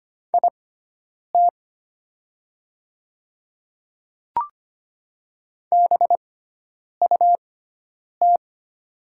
Morse code, heard as a keyed tone, about 25 wpm, spaced as for 9 wpm, 700 Hz. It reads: IT BUT
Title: 6x Character Spacing - Top 300 Words - 25wpm